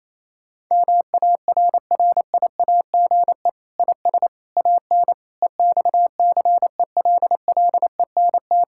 Morse code, heard as a keyed tone, about 28 wpm, 700 Hz. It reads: MARRIAGE IS AN EXCELLENT
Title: Sentences from Top 1000 Words - Review Part 5 - 28wpm